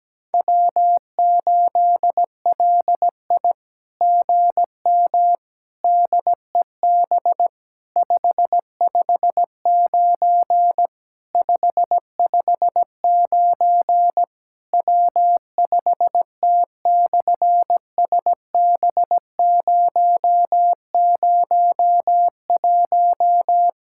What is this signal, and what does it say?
Morse code, heard as a keyed tone, about 17 wpm, 700 Hz. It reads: W8LI GM DEB 559 559 W5T/SB001
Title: SOTA QSO Practice 17wpm